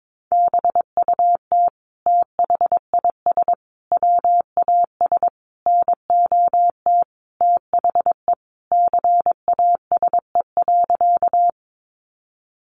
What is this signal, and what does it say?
Morse code, heard as a keyed tone, about 22 wpm, 700 Hz. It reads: BUT THIS WAS NOT THE CASE.
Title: Sentences from Top 700 Words 22wpm